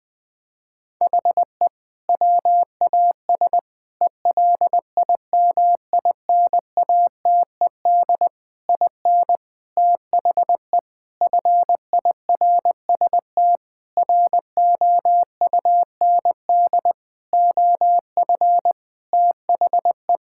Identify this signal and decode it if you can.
Morse code, heard as a keyed tone, about 21 wpm, 700 Hz. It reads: HE WAS ELIMINATED IN THE FIRST ROUND OF THE
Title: Sentences with 6-Word Phrases - Spoken Beginning - 20wpm